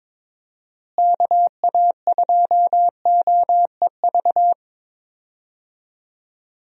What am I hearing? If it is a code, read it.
Morse code, heard as a keyed tone, about 21 wpm, 700 Hz. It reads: KA2OEV
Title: USA Call Signs - 6 Characters - Repeated 2x - 22wpm